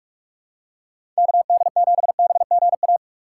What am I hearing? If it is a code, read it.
Morse code, heard as a keyed tone, about 42 wpm, 700 Hz. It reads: KD7BGA
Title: USA Call Signs - 6 Characters - 3x Character Spacing - 45wpm